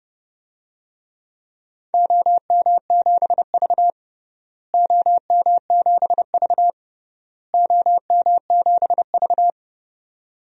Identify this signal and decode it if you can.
Morse code, heard as a keyed tone, about 31 wpm, 700 Hz. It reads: OM7V OM7V OM7V
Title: DXCC-100 Call Signs - 4 Characters - Repeated 3x - 30wpm